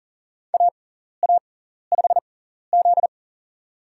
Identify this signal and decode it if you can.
Morse code, heard as a keyed tone, about 40 wpm, spaced as for 14 wpm, 700 Hz. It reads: AA5Z
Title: USA Call Signs - 4 Characters - 6x Character Spacing - 40wpm